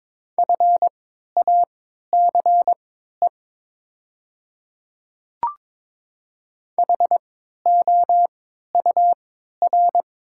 Morse code, heard as a keyed tone, about 22 wpm, spaced as for 13 wpm, 700 Hz. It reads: FACE HOUR
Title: 3x Character Spacing - Top 500 Words - 22wpm